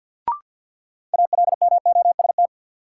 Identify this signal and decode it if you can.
Morse code, heard as a keyed tone, about 53 wpm, 700 Hz. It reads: ALMOST
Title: Sentences from Top 700 Words - New Word Drill - 50wpm